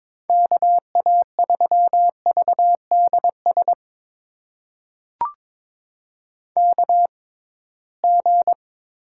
Morse code, heard as a keyed tone, about 23 wpm, spaced as for 17 wpm, 700 Hz. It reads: KA3VDS KG
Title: USA Call Signs - 6 Characters - 6x Character Spacing - 22wpm